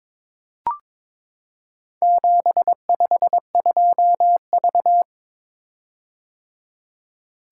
Morse code, heard as a keyed tone, about 22 wpm, 700 Hz. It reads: 752V
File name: practice / 22wpm / Four Letter-Number Combinations - 22wpm